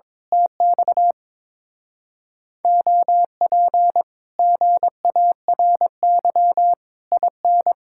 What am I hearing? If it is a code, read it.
Morse code, heard as a keyed tone, about 22 wpm, 700 Hz. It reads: TX OP GARY IN
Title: Speed-Racing Common QSO Phrases - 22wpm